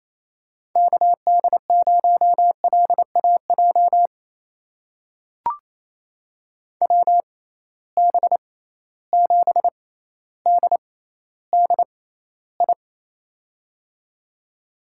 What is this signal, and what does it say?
Morse code, heard as a keyed tone, about 28 wpm, spaced as for 14 wpm, 700 Hz. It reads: KD0LAJ WB7DDI
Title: USA Call Signs - 6 Characters - 6x Character Spacing - 28wpm